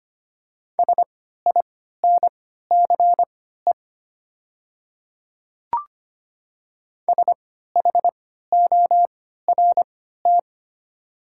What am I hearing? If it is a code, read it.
Morse code, heard as a keyed tone, about 25 wpm, spaced as for 14 wpm, 700 Hz. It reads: SINCE SHORT
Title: 3x Character Spacing - Top 500 Words - 25wpm